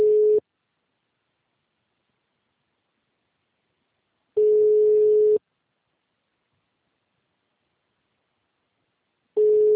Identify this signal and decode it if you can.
Ringback tone (Europe, 425 Hz).